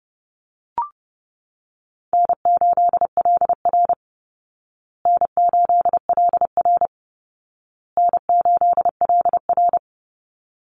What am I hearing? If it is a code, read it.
Morse code, heard as a keyed tone, about 30 wpm, 700 Hz. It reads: N8LR N8LR N8LR
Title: Easy Call Signs Repeated 3x - 30wpm